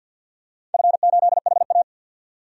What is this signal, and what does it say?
Morse code, heard as a keyed tone, about 51 wpm, 700 Hz. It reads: F8HA
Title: DXCC-100 Call Signs - 4 Characters - 50wpm